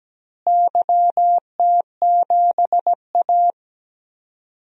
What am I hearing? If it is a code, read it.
Morse code, heard as a keyed tone, about 17 wpm, 700 Hz. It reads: YT7A
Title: DXCC-100 Call Signs - 4 Characters - Repeated 3x - 17wpm